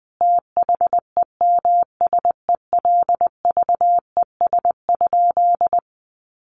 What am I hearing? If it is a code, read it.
Morse code, heard as a keyed tone, about 20 wpm, 700 Hz. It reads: THEMSELVES?
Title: Sentences from Top 800 Words 20wpm